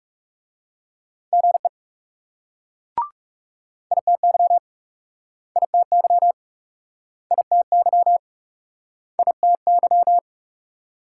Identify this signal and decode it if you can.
Morse code, heard as a keyed tone, about 40 wpm, 700 Hz. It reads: GE ITY ITY ITY ITY